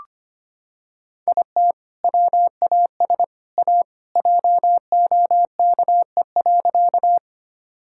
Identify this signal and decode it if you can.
Morse code, heard as a keyed tone, about 25 wpm, 700 Hz. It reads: IT WAS A JOKE.
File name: practice / 25wpm / Sentences with 3-Word Phrases - 25wpm